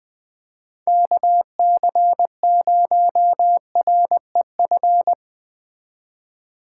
Morse code, heard as a keyed tone, about 20 wpm, 700 Hz. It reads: KC0REF